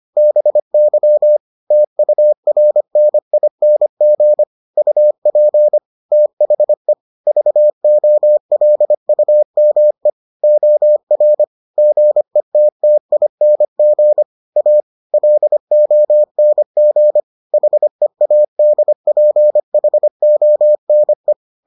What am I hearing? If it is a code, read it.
Morse code, heard as a keyed tone, about 25 wpm, 600 Hz. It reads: BY TURNING UP THE VOLUME OR GETTING A LONG HEADPHONE